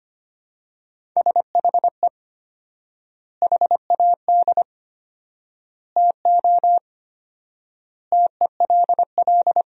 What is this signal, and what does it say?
Morse code, heard as a keyed tone, about 25 wpm, 700 Hz. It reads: SHE HAD TO TELL
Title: 4x Word-Spacing - Sentences from Top 200 Words - 25wpm